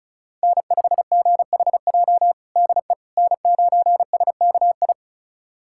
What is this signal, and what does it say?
Morse code, heard as a keyed tone, about 36 wpm, 700 Hz. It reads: N5GHJ DE N9SKI
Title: USA Callsign DE Callsign - 5 Characters - 35wpm